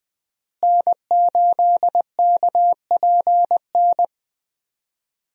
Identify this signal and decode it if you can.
Morse code, heard as a keyed tone, about 21 wpm, 700 Hz. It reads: N8KPN
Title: USA Call Signs - 5 Characters - Repeated 3x - 20wpm